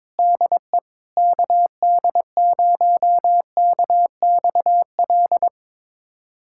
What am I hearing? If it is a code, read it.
Morse code, heard as a keyed tone, about 22 wpm, 700 Hz. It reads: DE KD0KXL
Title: USA Callsign DE Callsign 22wpm